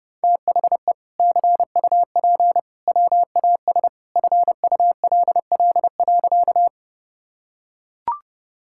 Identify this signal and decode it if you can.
Morse code, heard as a keyed tone, about 32 wpm, 700 Hz. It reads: THE CUP WAS FULL.
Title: Sentences from Top 1600 Words 30wpm